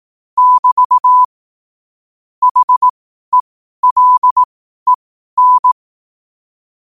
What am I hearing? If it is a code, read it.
Morse code, heard as a keyed tone, about 18 wpm, spaced as for 12 wpm, 1000 Hz. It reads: <BT> HELEN
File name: qotdfiles / 12wpm